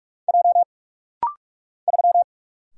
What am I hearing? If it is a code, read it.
Morse code, heard as a keyed tone, about 44 wpm, 700 Hz. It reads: J 3
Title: Single Letter-Number - Rapid-Fire 45wpm